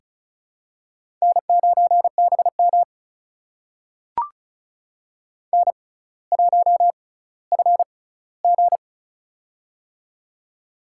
Morse code, heard as a keyed tone, about 34 wpm, spaced as for 18 wpm, 700 Hz. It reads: N9BM N1FG
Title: USA Call Signs - 4 Characters - 6x Character Spacing - 35wpm